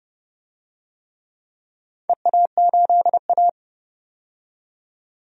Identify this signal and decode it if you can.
Morse code, heard as a keyed tone, about 31 wpm, 700 Hz. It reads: EA8A
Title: DXCC-100 Call Signs - 4 Characters - 30wpm